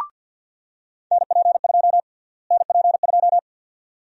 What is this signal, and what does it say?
Morse code, heard as a keyed tone, about 50 wpm, 700 Hz. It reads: NP2 NP2
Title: USA Call Sign Prefixes - Repeated 2x - 50wpm